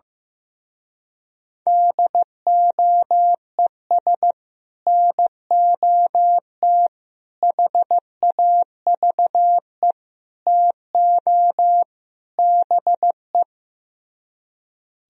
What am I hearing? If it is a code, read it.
Morse code, heard as a keyed tone, about 15 wpm, 700 Hz. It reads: DOES NOT HAVE TO BE
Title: Speed-Racing - Sets of 5 Words 15wpm